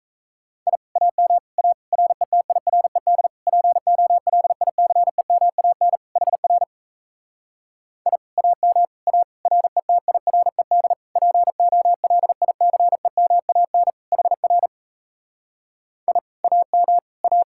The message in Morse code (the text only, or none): I AM A RETIRED POLICEMAN HR I AM A RETIRED POLICEMAN HR I AM A